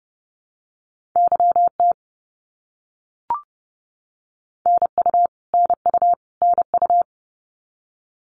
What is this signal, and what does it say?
Morse code, heard as a keyed tone, about 30 wpm, 700 Hz. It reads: YT NU NU NU
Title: Canadian Provinces and Territories - Repeated 3x - 30wpm